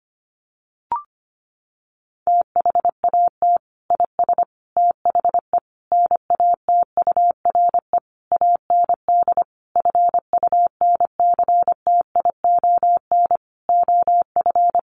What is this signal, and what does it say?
Morse code, heard as a keyed tone, about 25 wpm, 700 Hz. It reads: THAT IS THE NATURE AND FUNCTION OF
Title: Sentences from Top 1100 Words 25wpm